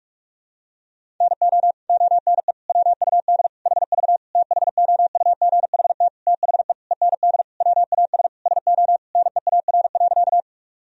Morse code, heard as a keyed tone, about 46 wpm, 700 Hz. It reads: NO ONE WUD HV THOUGHT THE END WAS SO NEAR.
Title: Sentences from Top 300 Words and Abbreviations 45wpm